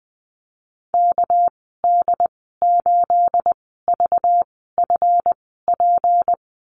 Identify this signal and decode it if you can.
Morse code, heard as a keyed tone, about 20 wpm, spaced as for 14 wpm, 700 Hz. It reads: KD8VFP
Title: 213 Lesson 34g - Number 8 - Call Signs 20wpm